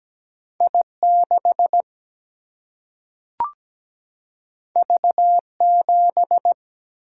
Morse code, heard as a keyed tone, about 17 wpm, 700 Hz. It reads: I 6 V 7
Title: Single-Letter Single-Number 17wpm